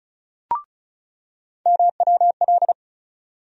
Morse code, heard as a keyed tone, about 36 wpm, 700 Hz. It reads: MWL
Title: USA Call Sign Postfixes - 3 Letters 35wpm